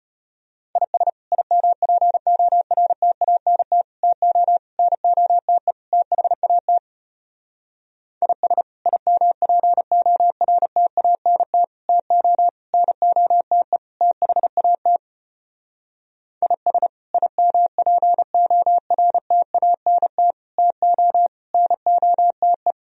Morse code, heard as a keyed tone, about 33 wpm, 700 Hz. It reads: IS IMPORTANT TO NOTE THAT IS IMPORTANT TO NOTE THAT IS IMPORTANT TO NOTE